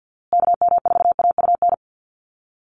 Morse code, heard as a keyed tone, about 48 wpm, 700 Hz. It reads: KM4AUN
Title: USA Call Signs Repeated 2x 50wpm